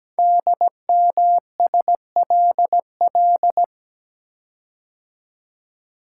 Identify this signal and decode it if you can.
Morse code, heard as a keyed tone, about 17 wpm, 700 Hz. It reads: DMSLL
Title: Code Groups - 5 Letter:Numbers - 17wpm